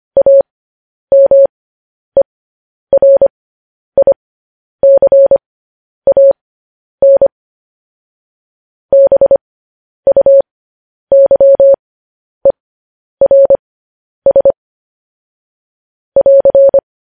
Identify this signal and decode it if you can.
Morse code, heard as a keyed tone, about 25 wpm, spaced as for 10 wpm, 550 Hz. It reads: AMERICAN BUYERS <AR>
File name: podcast / News headlines-1x-25x10